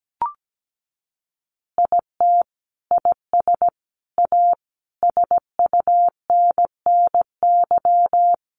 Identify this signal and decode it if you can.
Morse code, heard as a keyed tone, about 17 wpm, 700 Hz. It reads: IT IS A SUNNY
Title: Easy Phrases 17wpm